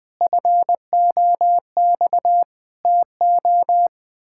Morse code, heard as a keyed tone, about 21 wpm, 700 Hz. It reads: FOX TO
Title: Pangram Challenge v2 - 20wpm